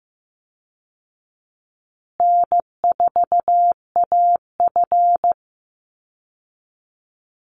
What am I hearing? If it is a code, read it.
Morse code, heard as a keyed tone, about 15 wpm, 700 Hz. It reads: N4AF
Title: Easy Call Signs Repeated 3x - 15wpm